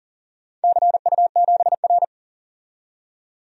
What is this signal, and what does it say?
Morse code, heard as a keyed tone, about 40 wpm, 700 Hz. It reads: CU7R
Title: DXCC-100 Call Signs - 4 Characters - 40wpm